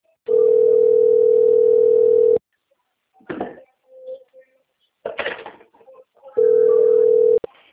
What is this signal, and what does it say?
Ringback tone (North America), 440 + 480 Hz.
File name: rg-1600-unknown- - - .9631